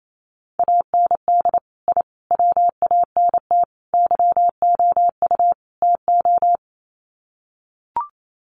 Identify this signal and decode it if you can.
Morse code, heard as a keyed tone, about 28 wpm, 700 Hz. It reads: AND I WANT YOU TO
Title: Sets of 5 Words Encore 28wpm